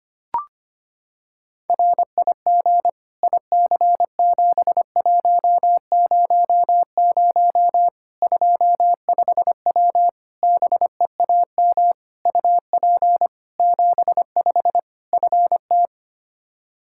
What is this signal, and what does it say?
Morse code, heard as a keyed tone, about 25 wpm, 700 Hz. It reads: RIG IC7100 25W BEAM UP 75 FT
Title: Practice QSO - Exchange 2 - Info - 25wpm